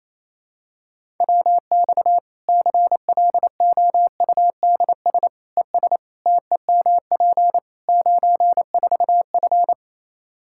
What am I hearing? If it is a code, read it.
Morse code, heard as a keyed tone, about 28 wpm, 700 Hz. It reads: WX CLOUDS ES TEMP 94F
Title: K1USN SST QSO Practice 28wpm